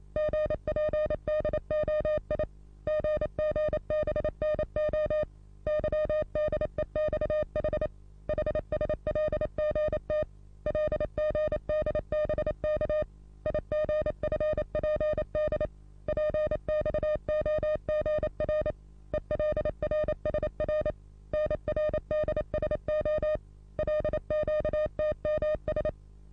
Morse code, heard as a keyed tone, about 29 wpm, 610 Hz. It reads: GPDOI GGBNO YDEXH HSLGT LGDBK IGFPD PXOGR ELRSR NRDSO LQTMS
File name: repos / Morsekurs / DJ4UF Morsen27